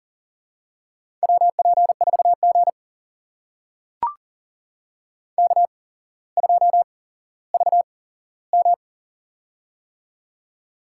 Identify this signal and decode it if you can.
Morse code, heard as a keyed tone, about 40 wpm, spaced as for 17 wpm, 700 Hz. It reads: WP4G K2VM
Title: USA Call Signs - 4 Characters - 8x Character Spacing - 40wpm